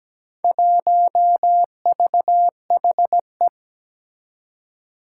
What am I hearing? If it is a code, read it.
Morse code, heard as a keyed tone, about 17 wpm, 700 Hz. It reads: 1VHE
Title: Four Challenging Letter-Number Combinations 17wpm